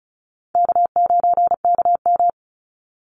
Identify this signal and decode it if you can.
Morse code, heard as a keyed tone, about 34 wpm, 700 Hz. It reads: K9KM
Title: Call Signs Repeated 3x - 35wpm